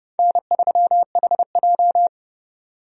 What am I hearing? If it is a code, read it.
Morse code, heard as a keyed tone, about 30 wpm, 700 Hz. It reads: N3HJ
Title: USA Call Signs - 4 Characters - 30wpm